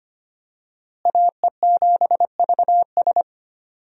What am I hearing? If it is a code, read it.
Morse code, heard as a keyed tone, about 25 wpm, 700 Hz. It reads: AE7VS
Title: USA Call Signs - 5 Characters - Repeated 2x - 25wpm